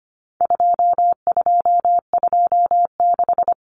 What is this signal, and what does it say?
Morse code, heard as a keyed tone, about 25 wpm, 700 Hz. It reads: 2226
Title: Common Cut Numbers - 4-Digit 25wpm